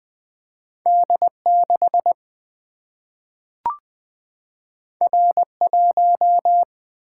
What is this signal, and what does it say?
Morse code, heard as a keyed tone, about 20 wpm, 700 Hz. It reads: D 6 R 1
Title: Single-Letter Single-Number 20wpm